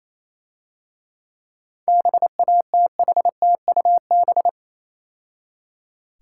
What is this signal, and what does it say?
Morse code, heard as a keyed tone, about 27 wpm, 700 Hz. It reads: BATHTUB
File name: Easy Words - Speed-Racing with Hints - 22wpm